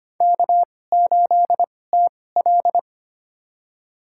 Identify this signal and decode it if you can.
Morse code, heard as a keyed tone, about 25 wpm, spaced as for 18 wpm, 700 Hz. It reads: K8TL
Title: USA Call Signs - 4 Characters - 2x Character Spacing - 25wpm